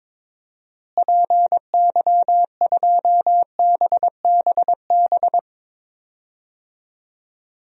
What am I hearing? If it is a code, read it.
Morse code, heard as a keyed tone, about 22 wpm, 700 Hz. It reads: PY2BBB
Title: DXCC-100 Call Signs - 6 Characters - Repeated 2x - 22wpm